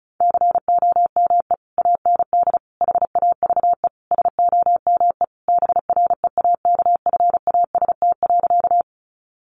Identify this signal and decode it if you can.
Morse code, heard as a keyed tone, about 34 wpm, 700 Hz. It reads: COME AND HAVE SOME BREAKFAST.
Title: Sentences from Top 1500 Words 35wpm